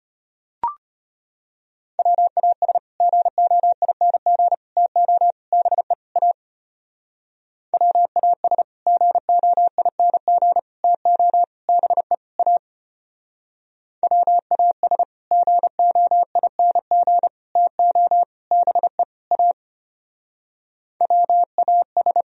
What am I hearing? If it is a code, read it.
Morse code, heard as a keyed tone, about 30 wpm, 700 Hz. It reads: WAS GOING TO BE A WAS GOING TO BE A WAS GOING TO BE A WAS
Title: Speed-Racing - Sets of 5 Words 25wpm